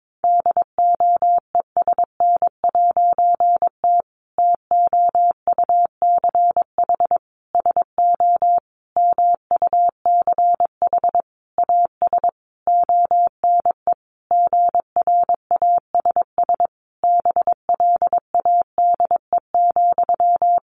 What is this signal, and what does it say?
Morse code, heard as a keyed tone, about 22 wpm, 700 Hz. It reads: DOESN'T TOUCH SO MUCH AS ONE GRASS BLADE,
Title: A Sound Of Thunder 22wpm